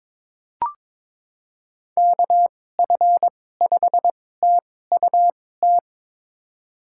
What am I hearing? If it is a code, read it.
Morse code, heard as a keyed tone, about 21 wpm, spaced as for 16 wpm, 700 Hz. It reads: KF5TUT
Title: USA Call Signs - 6 Characters - 2x Character Spacing - 22wpm